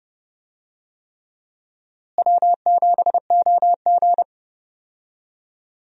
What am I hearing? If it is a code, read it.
Morse code, heard as a keyed tone, about 32 wpm, 700 Hz. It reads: W7OG
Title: USA Call Signs - 4 Characters - Repeated 3x - 30wpm